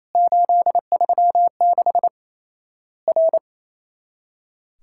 Morse code, heard as a keyed tone, about 27 wpm, 700 Hz. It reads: 836 R
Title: Worked All Europe DX Contest - QTC Exchange - 28wpm